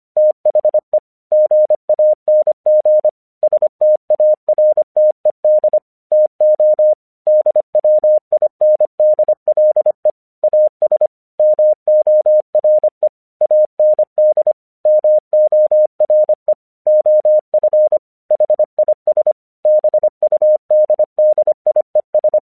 Morse code, heard as a keyed tone, about 25 wpm, 610 Hz. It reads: THE GANG STARTED TO DWINDLE AS MORE AND MORE OF HIS BUDDIES
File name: 1 English 25wpm / AQC13